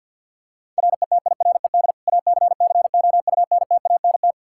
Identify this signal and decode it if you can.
Morse code, heard as a keyed tone, about 51 wpm, 700 Hz. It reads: RETIRED ACCOUNTANT
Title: K1USN SST QSO Practice 50wpm